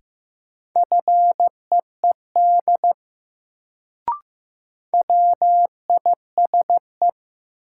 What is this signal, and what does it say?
Morse code, heard as a keyed tone, about 15 wpm, 700 Hz. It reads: FEED WISE
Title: Sentences from Top 1600 Words - New Word Drill Alt - 15wpm